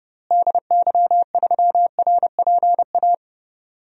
Morse code, heard as a keyed tone, about 30 wpm, 700 Hz. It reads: DY3RPA
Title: DXCC-100 Call Signs - 6 Characters - Repeated 2x - 30wpm